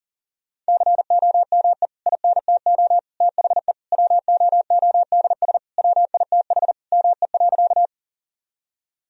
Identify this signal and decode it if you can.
Morse code, heard as a keyed tone, about 40 wpm, 700 Hz. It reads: COME INTO THE WOODS WITH ME.
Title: Sentences from Top 1000 Words - Review Part 2 - 40wpm